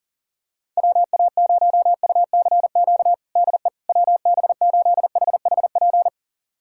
Morse code, heard as a keyed tone, about 40 wpm, 700 Hz. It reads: WA0UCQ DE WB8HHP
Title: USA Callsign DE Callsign - 6 Characters - 40wpm